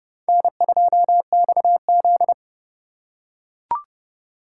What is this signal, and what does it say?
Morse code, heard as a keyed tone, about 30 wpm, 700 Hz. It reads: N2XZ